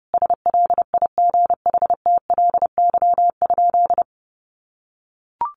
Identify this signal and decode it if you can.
Morse code, heard as a keyed tone, about 28 wpm, 700 Hz. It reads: SLIGHTLY?
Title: Sentences from Top 1300 Words 30wpm